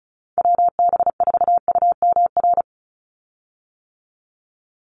Morse code, heard as a keyed tone, about 34 wpm, 700 Hz. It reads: WB4UMR
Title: USA Call Signs 35wpm